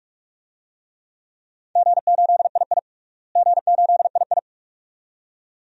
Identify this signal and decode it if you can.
Morse code, heard as a keyed tone, about 43 wpm, 700 Hz. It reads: G8II G8II
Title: DXCC-100 Call Signs - 4 Characters - Repeated 2x - 45wpm